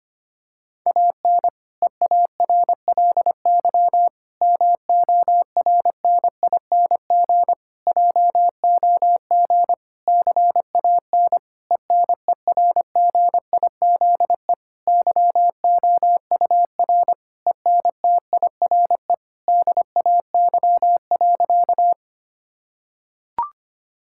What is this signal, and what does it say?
Morse code, heard as a keyed tone, about 25 wpm, 700 Hz. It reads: AN EARLY MORNING JOG CAN ENERGIZE YOUR ENTIRE DAY.
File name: practice / 25wpm / Difficult Sentences with Hints - 25wpm